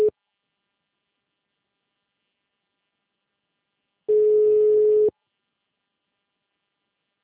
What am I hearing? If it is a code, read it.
Ringback tone (Europe, 425 Hz).